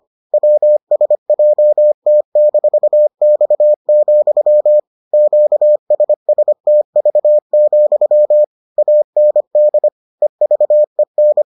Morse code, heard as a keyed tone, about 25 wpm, 600 Hz. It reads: WSJT-X, QSSTV, AND EVEN